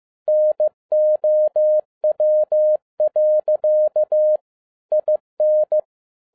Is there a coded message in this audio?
Yes: Morse code from 0.3 to 5.8 s, heard as a keyed tone, about 15 wpm, 610 Hz. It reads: NOW. IN